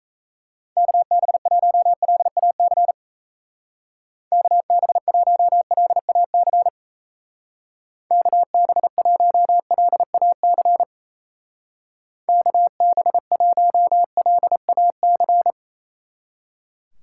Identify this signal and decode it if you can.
Morse code, heard as a keyed tone, about 34 wpm, 700 Hz. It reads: KB1LAC KB1LAC KB1LAC KB1LAC
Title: Speed Racing - USA 2x3 Call Signs 28wpm